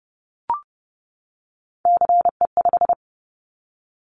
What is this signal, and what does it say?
Morse code, heard as a keyed tone, about 30 wpm, 700 Hz. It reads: CE5
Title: Callsign Prefixes for Top 100 DXCC - 30wpm